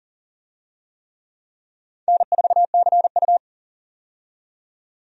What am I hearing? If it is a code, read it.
Morse code, heard as a keyed tone, about 40 wpm, 700 Hz. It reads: N4CU
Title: USA Call Signs - 4 Characters - 40wpm